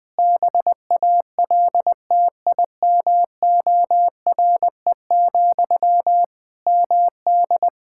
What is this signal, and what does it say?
Morse code, heard as a keyed tone, about 20 wpm, 700 Hz. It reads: BALTIMORE, MD
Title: Practice QSO - Exchange 1 - Beginning - 20wpm